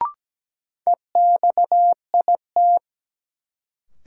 Morse code, heard as a keyed tone, about 17 wpm, 700 Hz. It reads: EXIT
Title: Sentences from Top 1100 Words - New Word Drill Alt - 17wpm